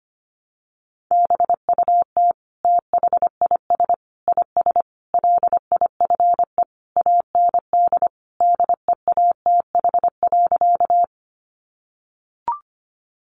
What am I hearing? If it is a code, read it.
Morse code, heard as a keyed tone, about 25 wpm, 700 Hz. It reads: BUT THIS IS LIFE AND DEATH.
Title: Sentences from Top 1100 Words 25wpm